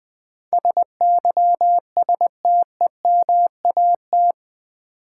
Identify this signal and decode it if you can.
Morse code, heard as a keyed tone, about 20 wpm, 700 Hz. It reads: SYSTEMAT
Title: Build A Word - 4 Letters At A Time - 10-Letter Words - 20wpm